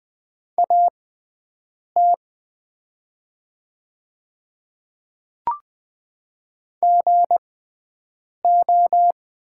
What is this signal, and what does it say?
Morse code, heard as a keyed tone, about 20 wpm, spaced as for 7 wpm, 700 Hz. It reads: AT GO
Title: 6x Character Spacing - Top 300 Words - 20wpm